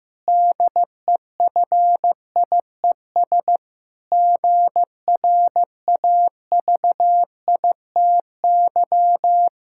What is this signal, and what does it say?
Morse code, heard as a keyed tone, about 15 wpm, 700 Hz. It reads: DEFIES GRAVITY
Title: Pangram Challenge v2 - 15wpm